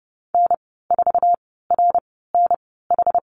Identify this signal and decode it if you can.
Morse code, heard as a keyed tone, about 30 wpm, spaced as for 17 wpm, 700 Hz. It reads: N4RNH
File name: Lesson 14g - Number 2 - Call Signs 30wpm